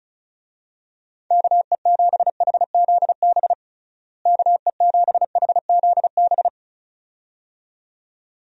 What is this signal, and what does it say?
Morse code, heard as a keyed tone, about 36 wpm, 700 Hz. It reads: KE7HZB KE7HZB
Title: USA Call Signs - 6 Characters - Repeated 2x - 35wpm